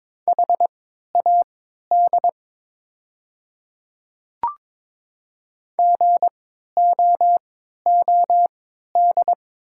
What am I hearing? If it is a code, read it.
Morse code, heard as a keyed tone, about 22 wpm, spaced as for 13 wpm, 700 Hz. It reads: HAD GOOD
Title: 3x Character Spacing - Top 200 Words - 22wpm